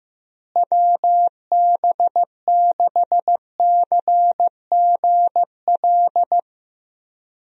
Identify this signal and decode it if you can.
Morse code, heard as a keyed tone, about 15 wpm, 700 Hz. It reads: WB6CGL
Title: USA Call Signs - 6 Characters - Repeated 2x - 15wpm